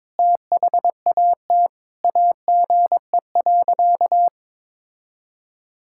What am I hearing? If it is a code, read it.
Morse code, heard as a keyed tone, about 22 wpm, 700 Hz. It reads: THAT AGE.
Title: Sentences from Top 1000 Words - Review Part 6 - 22wpm